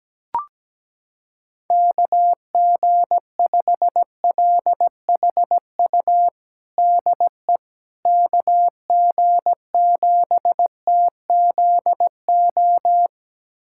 Morse code, heard as a keyed tone, about 17 wpm, 700 Hz. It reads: KG5LHU DE KG7TZO
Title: USA Callsign DE Callsign - 6 Characters - 17wpm